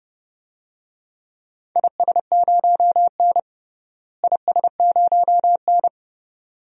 Morse code, heard as a keyed tone, about 32 wpm, 700 Hz. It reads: IS0N IS0N
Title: DXCC-100 Call Signs - 4 Characters - Repeated 2x - 30wpm